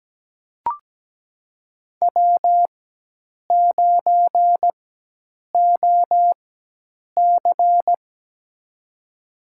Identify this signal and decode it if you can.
Morse code, heard as a keyed tone, about 17 wpm, spaced as for 8 wpm, 700 Hz. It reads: W9OC
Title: USA Call Signs - 4 Characters - 4x Character Spacing - 17wpm